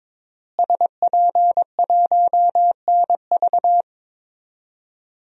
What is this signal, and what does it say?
Morse code, heard as a keyed tone, about 22 wpm, 700 Hz. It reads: SP1NV
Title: DXCC-100 Call Signs - 5 Characters - Repeated 3x - 22wpm